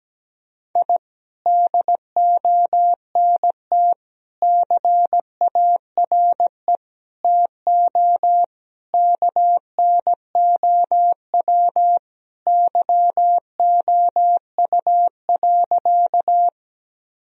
Morse code, heard as a keyed tone, about 17 wpm, 700 Hz. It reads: I DONT CARE TO KNOW YOU.
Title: Sentences from Top 1000 Words - Review Part 1 - 17wpm